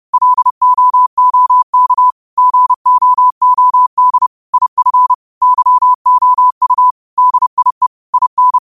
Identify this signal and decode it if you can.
Morse code, heard as a keyed tone, about 31 wpm, 1000 Hz. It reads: LOOK GOOD IF YOU DIE IN
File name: qotdfiles / 30wpm